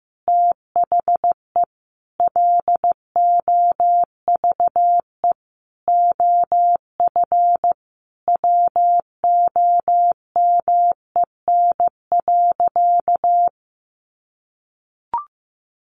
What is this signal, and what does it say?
Morse code, heard as a keyed tone, about 15 wpm, 700 Hz. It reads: THE LOVE OF WOMEN.
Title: Sentences from Top 1400 Words 15wpm